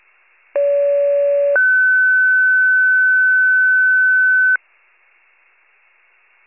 Two-tone page, heard as a one-second tone, then a three-second tone, about 570 then 1500 Hz. It is heard from 0.5 to 4.6 s.